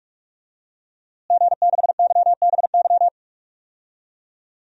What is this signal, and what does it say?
Morse code, heard as a keyed tone, about 46 wpm, 700 Hz. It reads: G6YBY
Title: DXCC-100 Call Signs - 5 Characters - Repeated 2x - 45wpm